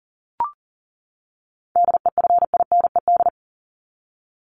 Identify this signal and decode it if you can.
Morse code, heard as a keyed tone, about 40 wpm, 700 Hz. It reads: DEFINED
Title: ED Suffix - Top 100 Words with Base Words 40wpm